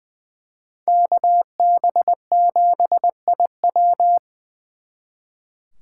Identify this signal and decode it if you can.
Morse code, heard as a keyed tone, about 24 wpm, 700 Hz. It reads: KB7IW
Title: Speed Racing - USA 2x2 Call Signs 20wpm